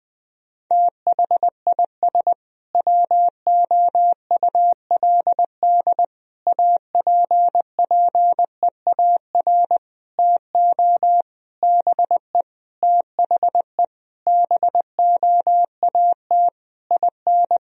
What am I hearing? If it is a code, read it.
Morse code, heard as a keyed tone, about 20 wpm, 700 Hz. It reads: THIS WOULD APPEAR TO BE THE BOAT IN
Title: Sentences from Top 1000 Words - Review Part 3 - 20wpm